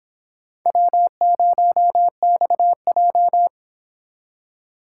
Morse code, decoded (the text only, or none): W0XJ